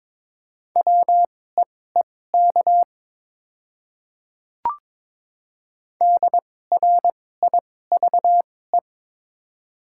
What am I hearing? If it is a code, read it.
Morse code, heard as a keyed tone, about 22 wpm, spaced as for 16 wpm, 700 Hz. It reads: WEEK DRIVE
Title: 2x Character Spacing - Top 500 Words - 22wpm